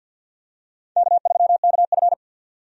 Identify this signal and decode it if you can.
Morse code, heard as a keyed tone, about 52 wpm, 700 Hz. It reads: K3KF
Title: USA Call Signs - 4 Characters - 50wpm